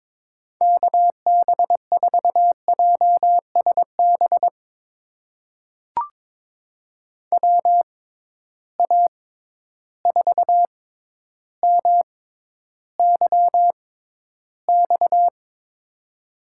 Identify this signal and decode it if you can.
Morse code, heard as a keyed tone, about 22 wpm, spaced as for 11 wpm, 700 Hz. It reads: KB4JSB WA4MYX
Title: USA Call Signs - 6 Characters - 6x Character Spacing - 22wpm